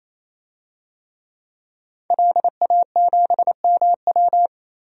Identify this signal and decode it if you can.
Morse code, heard as a keyed tone, about 28 wpm, 700 Hz. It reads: LA7MW
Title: DXCC-100 Call Signs - 5 Characters - 28wpm